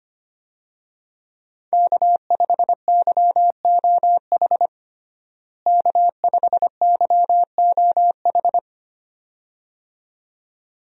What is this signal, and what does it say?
Morse code, heard as a keyed tone, about 25 wpm, 700 Hz. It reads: K5YOH K5YOH
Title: USA Call Signs - 5 Characters - Repeated 2x - 25wpm